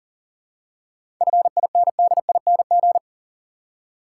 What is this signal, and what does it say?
Morse code, heard as a keyed tone, about 40 wpm, 700 Hz. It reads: FINDING
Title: Seven Letter Words - 40wpm